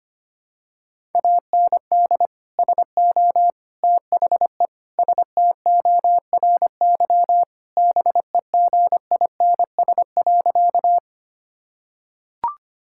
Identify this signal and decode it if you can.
Morse code, heard as a keyed tone, about 25 wpm, 700 Hz. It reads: AND SO THE STORY BEGINS.
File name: practice / 25wpm / Sentences with 2-Word Phrases - 25wpm